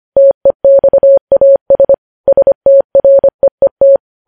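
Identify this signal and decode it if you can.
Morse code, heard as a keyed tone, about 25 wpm, 550 Hz. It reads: TEXAS STREET
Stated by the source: Morse practice sending news headlines at 25 wpm.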